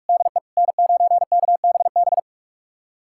Morse code, heard as a keyed tone, about 46 wpm, 700 Hz. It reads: DE N9KBB
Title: USA Callsign DE Callsign - 5 Characters - 45wpm